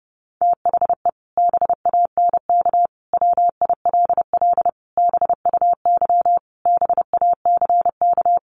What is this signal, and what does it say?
Morse code, heard as a keyed tone, about 30 wpm, 700 Hz. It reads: THE BANK WILL BUY BACK
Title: Sentences from Top 1500 Words 30wpm